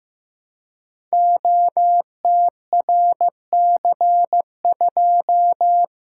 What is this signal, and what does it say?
Morse code, heard as a keyed tone, about 15 wpm, 700 Hz. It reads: OTRC2
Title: Code Groups - 5 Letter:Numbers - 15wpm